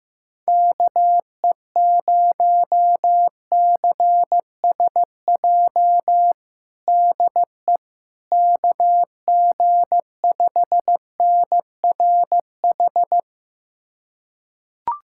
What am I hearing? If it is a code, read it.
Morse code, heard as a keyed tone, about 15 wpm, 700 Hz. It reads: KE0CSJ DE KG5NRH
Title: USA Callsign DE Callsign - 6 Characters - 15wpm